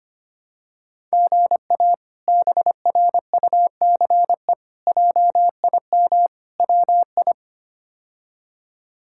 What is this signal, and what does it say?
Morse code, heard as a keyed tone, about 25 wpm, 700 Hz. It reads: GA BRUCE JIM WI
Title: K1USN SST QSO Practice - 25wpm